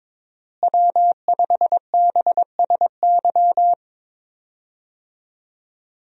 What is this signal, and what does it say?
Morse code, heard as a keyed tone, about 22 wpm, 700 Hz. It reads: W5BSY
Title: USA Call Signs - 5 Characters - Repeated 2x - 22wpm